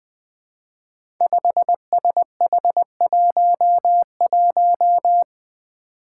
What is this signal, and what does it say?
Morse code, heard as a keyed tone, about 20 wpm, 700 Hz. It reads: 5SH11
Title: Five Challenging Letter-Number Combinations 20wpm